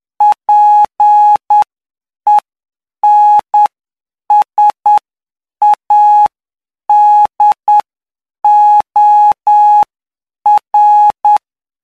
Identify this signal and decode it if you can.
Morse code, heard as a keyed tone, about 9 wpm, spaced as for 7 wpm, 840 Hz. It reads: PENSADOR
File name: cw / py2lcd / 05-PY2LCD-Li��o 14